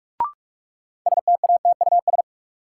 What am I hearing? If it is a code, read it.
Morse code, heard as a keyed tone, about 46 wpm, 700 Hz. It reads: STATUS
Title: Sentences from Top 1400 Words - New Word Drill - 45wpm